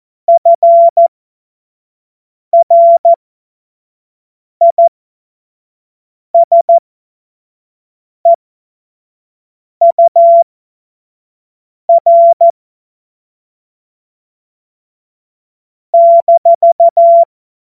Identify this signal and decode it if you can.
Morse code, heard as a keyed tone, about 14 wpm, spaced as for 5 wpm, 680 Hz. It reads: FRISEUR -
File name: Morsekurs / Witze / Witz 29. November 2022 25 F70